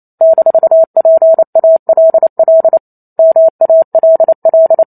Morse code, heard as a keyed tone, about 28 wpm, 650 Hz. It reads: -PALL MALL